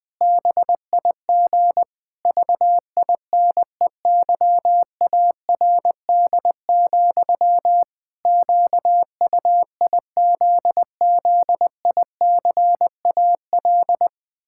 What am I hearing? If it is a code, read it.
Morse code, heard as a keyed tone, about 21 wpm, 700 Hz. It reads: BIG VINEYARD, QUIZZICAL
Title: Pangram Challenge v2 - 20wpm